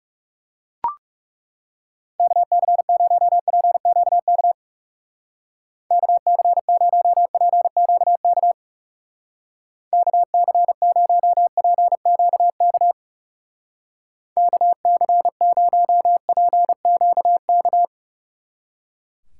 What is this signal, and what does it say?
Morse code, heard as a keyed tone, about 37 wpm, 700 Hz. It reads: KC0PQK KC0PQK KC0PQK KC0PQK
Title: Speed Racing - USA Call Signs 30wpm